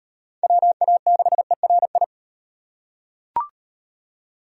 Morse code, heard as a keyed tone, about 35 wpm, 700 Hz. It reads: WA6ERI